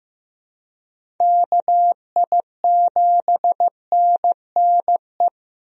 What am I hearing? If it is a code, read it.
Morse code, heard as a keyed tone, about 15 wpm, 700 Hz. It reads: KI7NNE
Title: USA Call Signs - 6 Characters - Repeated 2x - 15wpm